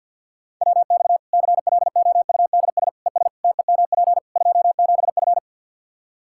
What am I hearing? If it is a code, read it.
Morse code, heard as a keyed tone, about 46 wpm, 700 Hz. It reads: WX CLOUDS ES TEMP 27F
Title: Practice QSO - Exchange 2 - Info - 50wpm